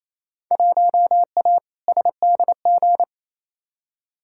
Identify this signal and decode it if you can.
Morse code, heard as a keyed tone, about 28 wpm, 700 Hz. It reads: 1A SDG
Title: Field Day Exchanges Encore - 28wpm